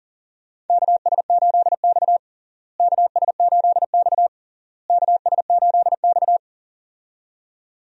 Morse code, heard as a keyed tone, about 40 wpm, 700 Hz. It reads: KS8X KS8X KS8X
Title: USA Call Signs - 4 Characters - Repeated 3x - 40wpm